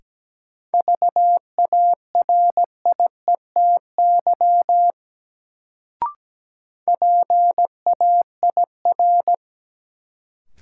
Morse code, heard as a keyed tone, about 17 wpm, 700 Hz. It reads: VARIETY PAIR
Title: Sets of 3 Words - New Word Drill - 17wpm